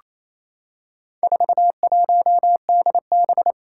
Morse code, heard as a keyed tone, about 28 wpm, 700 Hz. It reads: 41DB
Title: Four Challenging Letter-Number Combinations 28wpm